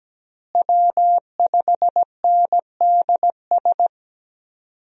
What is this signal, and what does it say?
Morse code, heard as a keyed tone, about 17 wpm, 700 Hz. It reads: W5NDS
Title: USA Call Signs - 5 Characters - Repeated 2x - 17wpm